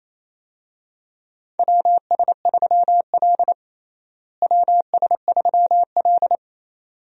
Morse code, heard as a keyed tone, about 27 wpm, 700 Hz. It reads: WS3L WS3L
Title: USA Call Signs - 4 Characters - Repeated 2x - 28wpm